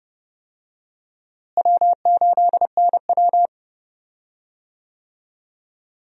Morse code, heard as a keyed tone, about 30 wpm, 700 Hz. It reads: W8NW
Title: USA Call Signs - 4 Characters - Repeated 3x - 30wpm